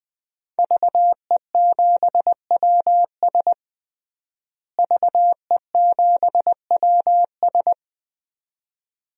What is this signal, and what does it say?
Morse code, heard as a keyed tone, about 20 wpm, 700 Hz. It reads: VE7WS VE7WS
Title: DXCC-100 Call Signs - 5 Characters - Repeated 2x - 20wpm